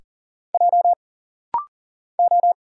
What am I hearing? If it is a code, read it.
Morse code, heard as a keyed tone, about 39 wpm, 700 Hz. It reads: J O
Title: Single Letters - Rapid-Fire 40wpm